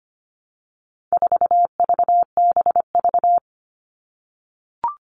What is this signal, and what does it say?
Morse code, heard as a keyed tone, about 25 wpm, 700 Hz. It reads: 4VBV
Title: All Cut Numbers - 4-Digit 25wpm